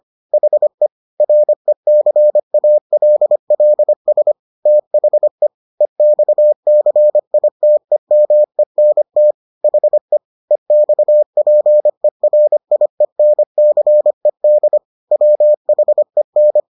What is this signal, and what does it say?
Morse code, heard as a keyed tone, about 25 wpm, 600 Hz. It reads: HE RECALLS THE EXCITEMENT HE EXPERIENCED WHEN